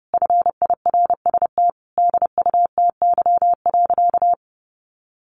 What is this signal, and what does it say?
Morse code, heard as a keyed tone, about 30 wpm, 700 Hz. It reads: FIRST DUTY.
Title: Sentences from Top 1500 Words 30wpm